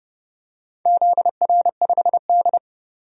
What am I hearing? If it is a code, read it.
Morse code, heard as a keyed tone, about 29 wpm, 700 Hz. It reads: ZR5D
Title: DXCC-100 Call Signs - 4 Characters - 30wpm